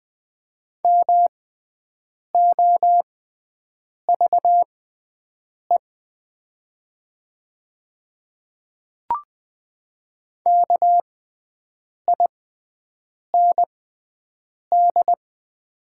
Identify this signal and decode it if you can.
Morse code, heard as a keyed tone, about 20 wpm, spaced as for 7 wpm, 700 Hz. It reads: MOVE KIND
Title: 6x Character Spacing - Top 200 Words - 20wpm